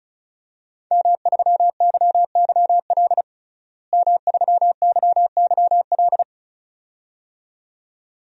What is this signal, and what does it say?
Morse code, heard as a keyed tone, about 34 wpm, 700 Hz. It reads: M3YYL M3YYL
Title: DXCC-100 Call Signs - 5 Characters - Repeated 2x - 35wpm